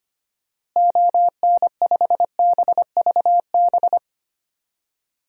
Morse code, heard as a keyed tone, about 25 wpm, 700 Hz. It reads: ON5BVB
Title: DXCC-100 Call Signs - 6 Characters - Repeated 3x - 25wpm